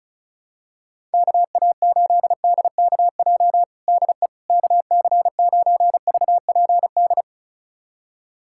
Morse code, heard as a keyed tone, about 36 wpm, 700 Hz. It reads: KA8DKJ DE KC9VPD
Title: USA Callsign DE Callsign - 6 Characters - 35wpm